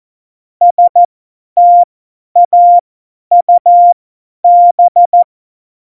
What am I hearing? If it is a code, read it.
Morse code, heard as a keyed tone, about 14 wpm, spaced as for 10 wpm, 700 Hz. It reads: STAUB
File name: Morsekurs / Witze / Witz 12. November 2019, 50 F70